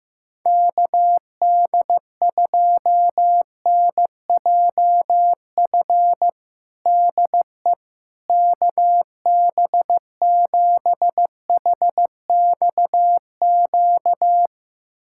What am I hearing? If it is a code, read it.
Morse code, heard as a keyed tone, about 16 wpm, 700 Hz. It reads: KD2NJF DE KB7HXQ
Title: USA Callsign DE Callsign - 6 Characters - 15wpm